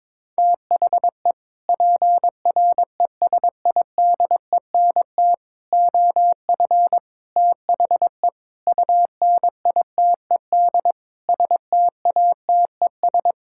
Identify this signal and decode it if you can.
Morse code, heard as a keyed tone, about 22 wpm, 700 Hz. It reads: THE PRESIDENT OF THE UNITED STATES
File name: practice / 22wpm / Sets of 6 Words - 22wpm